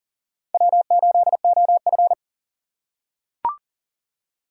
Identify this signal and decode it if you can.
Morse code, heard as a keyed tone, about 40 wpm, 700 Hz. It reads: W8OF
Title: USA Call Signs - 4 Characters - 4x Character Spacing - 40wpm